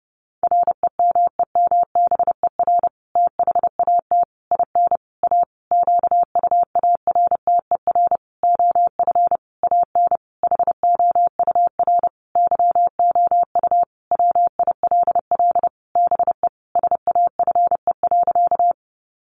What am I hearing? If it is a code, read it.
Morse code, heard as a keyed tone, about 30 wpm, 700 Hz. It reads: REMEMBER THAT IN A QUARTER OF AN HOUR YOU WILL BE SAFE.
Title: Sentences from Top 1300 Words 30wpm